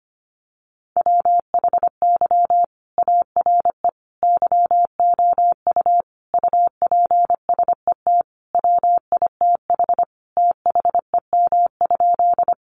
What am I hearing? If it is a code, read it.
Morse code, heard as a keyed tone, about 25 wpm, 700 Hz. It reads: WHY ARE YOU UPSET WITH THEM?
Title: Sentences from Top 800 Words 25wpm